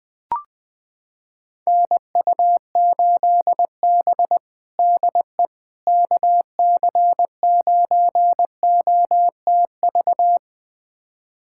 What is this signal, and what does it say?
Morse code, heard as a keyed tone, about 20 wpm, 700 Hz. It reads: NU8B DE KC9OTV
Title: USA Callsign DE Callsign 20wpm